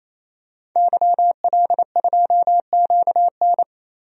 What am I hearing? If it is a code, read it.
Morse code, heard as a keyed tone, about 27 wpm, 700 Hz. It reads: YL2QN
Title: DXCC-100 Call Signs - 5 Characters - 28wpm